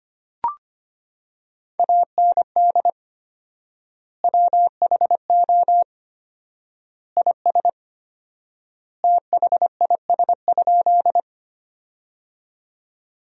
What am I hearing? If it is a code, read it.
Morse code, heard as a keyed tone, about 25 wpm, 700 Hz. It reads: AND WHO IS THIS?
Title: 4x Word-Spacing - Sentences from Top 100 Words - 25wpm